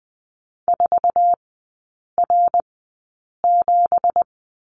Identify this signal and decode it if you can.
Morse code, heard as a keyed tone, about 20 wpm, spaced as for 8 wpm, 700 Hz. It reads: 4R7
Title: 184 Lesson 30e - Letter b - Three Characters 20wpm